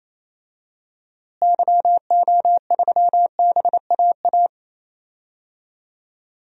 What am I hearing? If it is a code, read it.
Morse code, heard as a keyed tone, about 28 wpm, 700 Hz. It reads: YO3BAA
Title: DXCC-100 Call Signs - 6 Characters - Repeated 3x - 28wpm